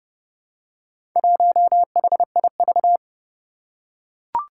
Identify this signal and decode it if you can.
Morse code, heard as a keyed tone, about 30 wpm, 700 Hz. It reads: 1HIV
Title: Four Challenging Letter-Number Combinations 30wpm